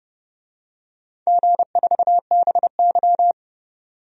Morse code, heard as a keyed tone, about 30 wpm, 700 Hz. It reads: G4BY
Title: DXCC-100 Call Signs - 4 Characters - 30wpm